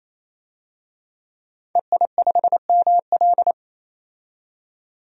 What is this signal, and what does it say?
Morse code, heard as a keyed tone, about 28 wpm, 700 Hz. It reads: EI5ML
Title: DXCC-100 Call Signs - 5 Characters - 28wpm